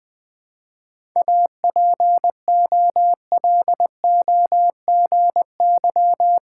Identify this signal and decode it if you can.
Morse code, heard as a keyed tone, about 20 wpm, 690 Hz. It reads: APOLOGY
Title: 7 Letter Words - First 3 Letters Given - 20wpm